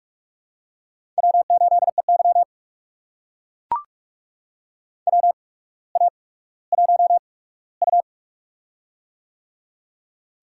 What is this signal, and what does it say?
Morse code, heard as a keyed tone, about 44 wpm, spaced as for 19 wpm, 700 Hz. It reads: W8EY WA1U
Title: USA Call Signs - 4 Characters - 8x Character Spacing - 45wpm